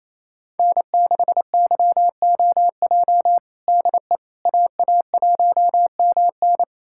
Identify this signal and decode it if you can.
Morse code, heard as a keyed tone, about 27 wpm, 700 Hz. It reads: N6YOJ DE AA1MN
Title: USA Callsign DE Callsign - 5 Characters - 28wpm